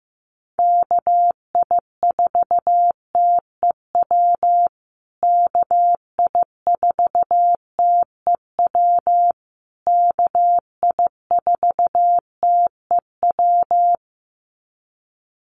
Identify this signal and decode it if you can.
Morse code, heard as a keyed tone, about 15 wpm, 700 Hz. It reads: KI4TEW KI4TEW KI4TEW
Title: USA Call Signs Repeated 3x 15wpm